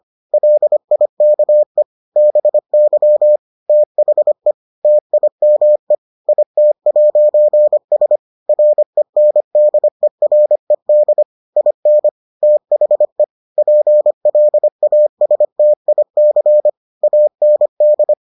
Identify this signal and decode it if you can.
Morse code, heard as a keyed tone, about 25 wpm, 600 Hz. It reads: LIKE BY THE TIME IT'S RENDERED IN THE PLASTIC AND